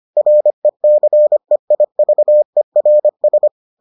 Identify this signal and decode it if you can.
Morse code, heard as a keyed tone, about 25 wpm, 600 Hz. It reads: RECEIVERS